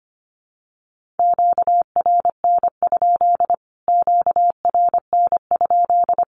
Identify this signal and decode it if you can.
Morse code, heard as a keyed tone, about 25 wpm, 700 Hz. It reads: QRN? QRN?
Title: Q-Codes Repeated 3x 25wpm